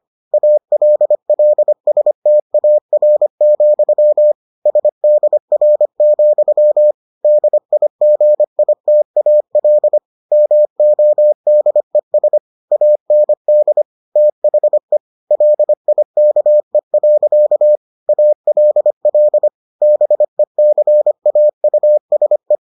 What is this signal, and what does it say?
Morse code, heard as a keyed tone, about 25 wpm, 600 Hz. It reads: ALLSTAR, SDR, DIGITAL MODES AND THE LIKE. ALL BECAUSE